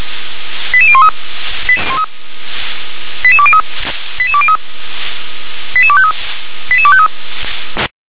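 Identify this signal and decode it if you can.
Five-tone selective call (ZVEI), heard as quick runs of five tones, tones of about 70 ms. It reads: 88013 88203 88203 88253 88253
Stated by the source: ZVEI five-tone selective call received via SDR.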